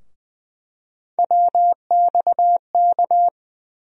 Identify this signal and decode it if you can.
Morse code, heard as a keyed tone, about 20 wpm, 700 Hz. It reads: WXK
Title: USA Call Sign Postfixes - 3 Letters 20wpm